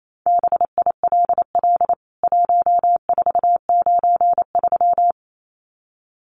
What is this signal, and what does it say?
Morse code, heard as a keyed tone, about 28 wpm, 700 Hz. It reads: BILL 1493
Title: CWT Exchange 28wpm